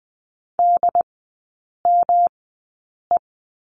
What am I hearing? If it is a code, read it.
Morse code, heard as a keyed tone, about 20 wpm, spaced as for 8 wpm, 700 Hz. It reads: DME
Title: 133 Lesson 22e - Question Mark - Three Characters 20wpm